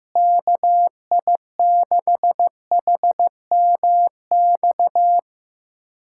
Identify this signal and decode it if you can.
Morse code, heard as a keyed tone, about 15 wpm, 700 Hz. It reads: KI6HMX
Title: USA Callsign DE Callsign 15wpm